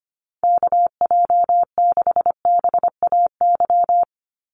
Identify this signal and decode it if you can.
Morse code, heard as a keyed tone, about 25 wpm, 700 Hz. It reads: KJ6BAY
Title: USA Call Signs 25wpm